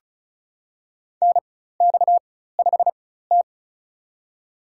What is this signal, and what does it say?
Morse code, heard as a keyed tone, about 36 wpm, spaced as for 16 wpm, 700 Hz. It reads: NX5T
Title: USA Call Signs - 4x Character Spacing - 35wpm